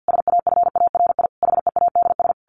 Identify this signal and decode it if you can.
Morse code, heard as a keyed tone, about 50 wpm, 700 Hz. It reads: SAFARI HEADS
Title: A Sound Of Thunder 50wpm